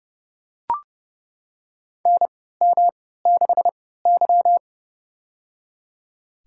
Morse code, heard as a keyed tone, about 30 wpm, spaced as for 17 wpm, 700 Hz. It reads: NM6Y
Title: USA Call Signs - 4 Characters - 3x Character Spacing - 30wpm